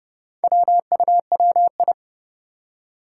Morse code, heard as a keyed tone, about 30 wpm, 700 Hz. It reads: WUWI
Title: Four Challenging Letter-Number Combinations 30wpm